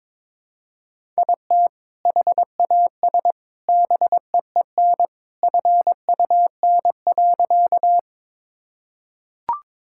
Morse code, heard as a keyed tone, about 22 wpm, 700 Hz. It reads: IT HAS BEEN FUN.
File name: practice / 22wpm / Sentences with 2-Word Phrases - 22wpm